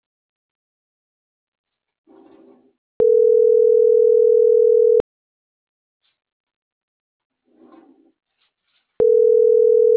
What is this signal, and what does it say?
Ringback tone (North America), 440 + 480 Hz.